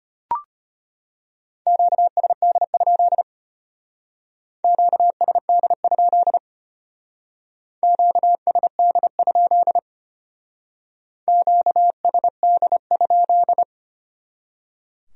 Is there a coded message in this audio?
Yes: Morse code from 1.7 to 13.6 s, heard as a keyed tone, about 32 wpm, 700 Hz. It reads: QSD? QSD? QSD? QSD?